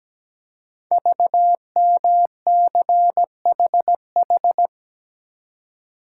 Morse code, heard as a keyed tone, about 17 wpm, 700 Hz. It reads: VMCHH
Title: Code Groups - 5 Letter:Numbers - 17wpm